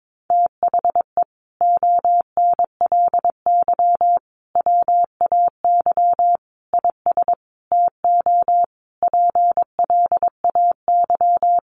Morse code, heard as a keyed tone, about 22 wpm, 700 Hz. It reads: THE ONLY WAY IS TO PLAY
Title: Sentences from Top 1500 Words 22wpm